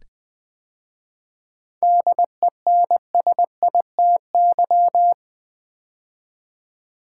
Morse code, heard as a keyed tone, about 20 wpm, 700 Hz. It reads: DENSITY
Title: 7 Letter Words - First 3 Letters Given - 20wpm